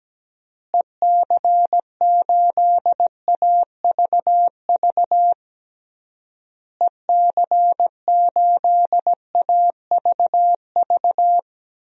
Morse code, heard as a keyed tone, about 17 wpm, 700 Hz. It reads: EC8AVV EC8AVV
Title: DXCC-100 Call Signs - 6 Characters - Repeated 3x - 17wpm